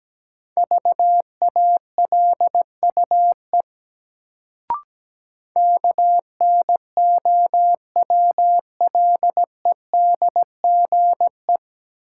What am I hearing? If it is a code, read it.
Morse code, heard as a keyed tone, about 17 wpm, 700 Hz. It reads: VALUE KNOWLEDGE
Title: Sentences from Top 600 Words - New Word Drill - 17wpm